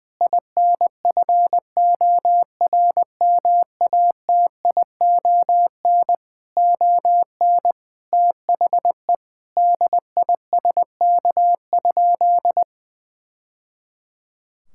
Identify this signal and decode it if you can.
Morse code, heard as a keyed tone, about 20 wpm, 700 Hz. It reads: INFORMATION ON THE DISK?
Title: Sentences from Top 1000 Words - Review Part 6 - 20wpm